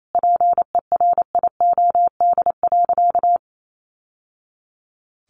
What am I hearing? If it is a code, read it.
Morse code, heard as a keyed tone, about 28 wpm, 700 Hz. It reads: PERIOD.
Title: Sentences from Top 800 Words 28wpm